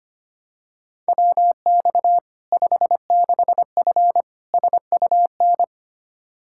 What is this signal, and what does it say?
Morse code, heard as a keyed tone, about 25 wpm, 700 Hz. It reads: WX 56F SUN
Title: Practice QSO - Exchange 2 - Info - 25wpm